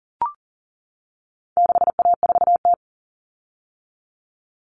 Morse code, heard as a keyed tone, about 40 wpm, 700 Hz. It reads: 6A4T
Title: Common Cut Numbers - 4-Digit 40wpm